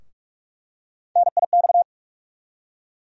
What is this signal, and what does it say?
Morse code, heard as a keyed tone, about 43 wpm, 700 Hz. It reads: NIX
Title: USA Call Sign Postfixes - 3 Letters 45wpm